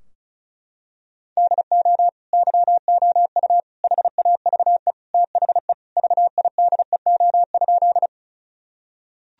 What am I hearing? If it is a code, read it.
Morse code, heard as a keyed tone, about 36 wpm, 700 Hz. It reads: DO YOU HAVE THE VIDEO?
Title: Sentences from Top 1000 Words - Review Part 4 - 35wpm